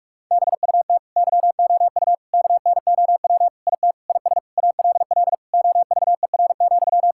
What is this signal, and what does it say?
Morse code, heard as a keyed tone, about 41 wpm, 700 Hz. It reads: BUT YOU KNOW IT IS ALL OVER,
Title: Sentences from Top 1600 Words 45wpm